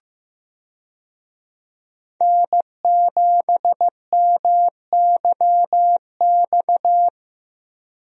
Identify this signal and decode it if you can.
Morse code, heard as a keyed tone, about 15 wpm, 700 Hz. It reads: N7MYX
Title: USA Call Signs - 5 Characters - Repeated 2x - 15wpm